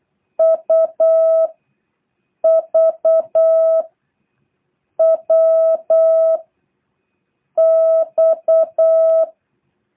Morse code, heard as a keyed tone, about 8 wpm, 650 Hz. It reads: U V W X
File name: jh8cbh / 2023cw